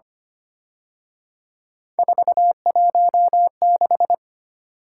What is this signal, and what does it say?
Morse code, heard as a keyed tone, about 31 wpm, 700 Hz. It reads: 416
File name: Speed-Racing - Three Letter-Number Combinations - 25wpm